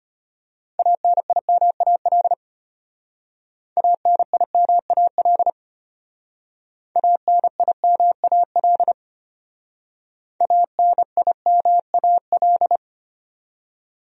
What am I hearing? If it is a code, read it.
Morse code, heard as a keyed tone, about 32 wpm, 700 Hz. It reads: ANIMAL ANIMAL ANIMAL ANIMAL